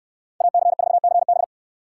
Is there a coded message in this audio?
Yes: Morse code from 0.4 to 1.4 s, heard as a keyed tone, about 66 wpm, 700 Hz. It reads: AC4CB